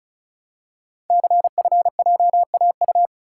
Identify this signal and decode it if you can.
Morse code, heard as a keyed tone, about 36 wpm, 690 Hz. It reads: CFJAU
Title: Code Groups - 5 Letters - 35wpm